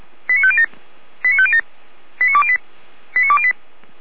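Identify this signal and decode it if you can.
Five-tone selective call (ZVEI), heard as quick runs of five tones, tones of about 70 ms. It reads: 89507 89507 89208 89208
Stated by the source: ZVEI five-tone call 89220 received on SDR, logged by feuerwehr-orlamuende fire brigade server.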